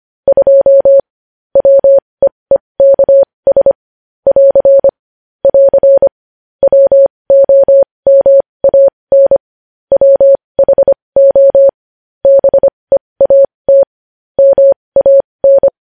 Morse code, heard as a keyed tone, about 25 wpm, spaced as for 20 wpm, 550 Hz. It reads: 2 WEEKS <AR> <AR> WOMAN WHO BEAT MAN
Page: news headlines in morse code at 20 wpm